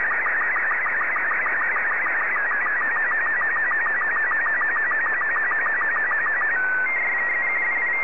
SSTV (slow-scan television), Robot 36.
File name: sstv / R36-VK3DFL